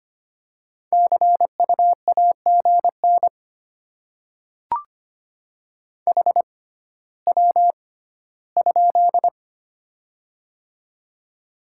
Morse code, heard as a keyed tone, about 25 wpm, spaced as for 14 wpm, 700 Hz. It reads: CU AGN HW?
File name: Top 20 QSO Elements - 6x Character Spacing - 25wpm